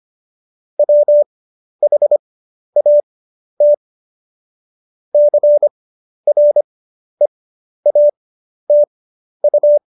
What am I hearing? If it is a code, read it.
Morse code, heard as a keyed tone, about 25 wpm, spaced as for 11 wpm, 590 Hz. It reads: WHAT CREATU
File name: Jokes-25-char-10-eff